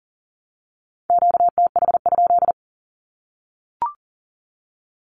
Morse code, heard as a keyed tone, about 40 wpm, 700 Hz. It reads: QTH?
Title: Q-Codes Repeated 3x 40wpm